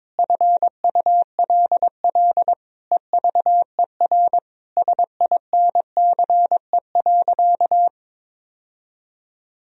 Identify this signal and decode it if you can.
Morse code, heard as a keyed tone, about 21 wpm, 700 Hz. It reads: FULL EVER SINCE.
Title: Sentences from Top 1000 Words - Review Part 3 - 22wpm